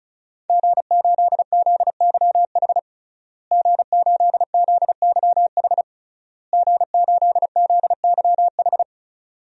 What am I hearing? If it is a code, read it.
Morse code, heard as a keyed tone, about 34 wpm, 700 Hz. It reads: G8ZYH G8ZYH G8ZYH
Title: DXCC-100 Call Signs - 5 Characters - Repeated 3x - 35wpm